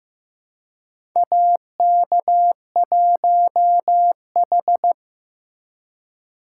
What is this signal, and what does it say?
Morse code, heard as a keyed tone, about 15 wpm, 700 Hz. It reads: AK1H
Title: USA Call Signs - 4 Characters - 3x Character Spacing - 15wpm